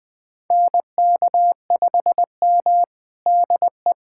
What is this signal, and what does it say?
Morse code, heard as a keyed tone, about 20 wpm, 700 Hz. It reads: NK5M DE
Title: USA Callsign DE Callsign - 4 Characters - 20wpm